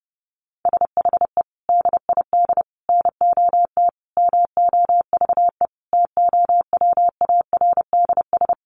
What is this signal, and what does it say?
Morse code, heard as a keyed tone, about 30 wpm, 700 Hz. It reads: SHE DID NOT MOVE TOWARDS
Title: Sentences from Top 1500 Words 30wpm